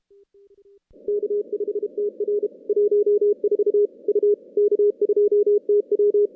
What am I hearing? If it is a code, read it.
Morse code, heard as a keyed tone, about 32 wpm, 410 Hz. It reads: K5TR 14 U K2TW
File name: contests / CW-SS-k5tr- -2100